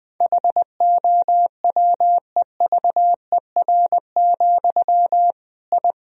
Morse code, heard as a keyed tone, about 21 wpm, 700 Hz. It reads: HOWEVER, I
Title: Sentences from Top 1000 Words - Review Part 3 - 20wpm